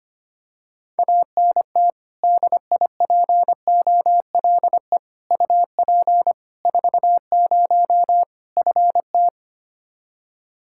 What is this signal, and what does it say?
Morse code, heard as a keyed tone, about 25 wpm, 700 Hz. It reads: ANT DIPOLE UP 40 FT
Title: K1USN SST QSO Practice 25wpm